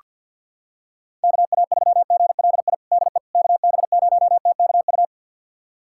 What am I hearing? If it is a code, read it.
Morse code, heard as a keyed tone, about 47 wpm, 690 Hz. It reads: KA3GLI DE KB0TKU
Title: USA Callsign DE Callsign - 6 Characters - 50wpm